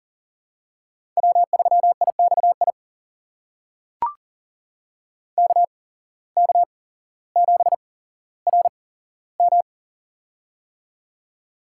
Morse code, heard as a keyed tone, about 38 wpm, spaced as for 17 wpm, 700 Hz. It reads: W3IXI KK7RM
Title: USA Call Signs - 5 Characters - 8x Character Spacing - 40wpm